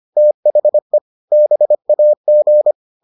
Morse code, heard as a keyed tone, about 25 wpm, 600 Hz. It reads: THE BAG